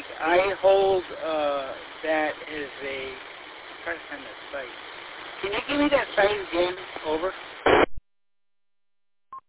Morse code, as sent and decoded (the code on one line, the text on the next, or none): none